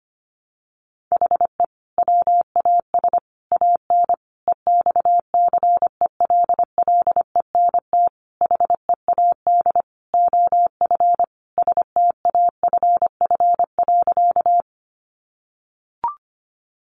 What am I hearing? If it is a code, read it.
Morse code, heard as a keyed tone, about 25 wpm, 700 Hz. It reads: HE WAS AN EXCELLENT HEAD OF STAFF.
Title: Sentences from Top 1200 Words 25wpm